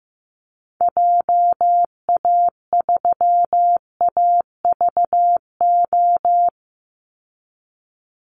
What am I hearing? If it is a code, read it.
Morse code, heard as a keyed tone, about 15 wpm, 700 Hz. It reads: JA3AVO
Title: Call Signs Repeated 3x - 15wpm